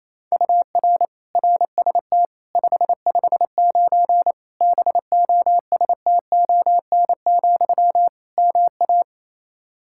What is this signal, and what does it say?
Morse code, heard as a keyed tone, about 28 wpm, 700 Hz. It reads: UR RST 559 BOSTON, MA
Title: K1USN SST QSO Practice 28wpm